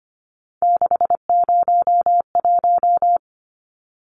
Morse code, heard as a keyed tone, about 25 wpm, 700 Hz. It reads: 601
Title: Numbers - 3-Digit 25wpm